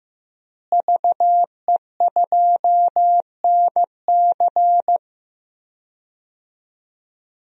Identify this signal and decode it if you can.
Morse code, heard as a keyed tone, about 15 wpm, 700 Hz. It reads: VE2NC
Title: DXCC-100 Call Signs - 5 Characters - Repeated 2x - 15wpm